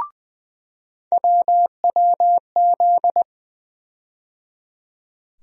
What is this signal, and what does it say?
Morse code, heard as a keyed tone, about 20 wpm, 700 Hz. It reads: WWZ
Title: USA Call Sign Postfixes - 3 Letters 20wpm